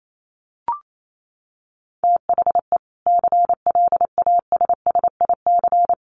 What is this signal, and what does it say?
Morse code, heard as a keyed tone, about 28 wpm, 700 Hz. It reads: THE CLASSIC
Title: Sentences from Top 1000 Words 28wpm